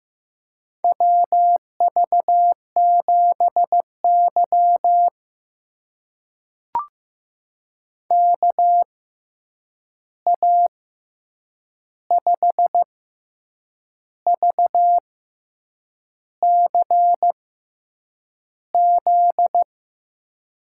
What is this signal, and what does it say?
Morse code, heard as a keyed tone, about 15 wpm, spaced as for 7 wpm, 700 Hz. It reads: WV7Y KA5VCZ
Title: USA Call Signs - 6x Character Spacing - 15wpm